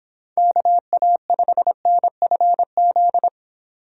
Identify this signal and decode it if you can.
Morse code, heard as a keyed tone, about 24 wpm, 700 Hz. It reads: KA5NFZ